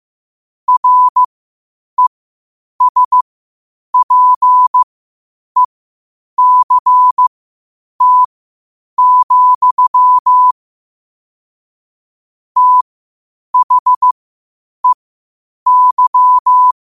Morse code, heard as a keyed tone, about 15 wpm, spaced as for 8 wpm, 1000 Hz. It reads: RESPECT, THEY
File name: qotdfiles / 8wpm